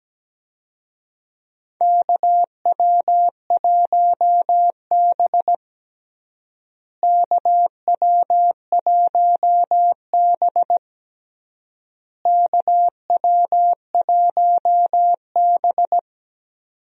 Morse code, heard as a keyed tone, about 17 wpm, 700 Hz. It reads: KW1B KW1B KW1B
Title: USA Call Signs - 4 Characters - Repeated 3x - 17wpm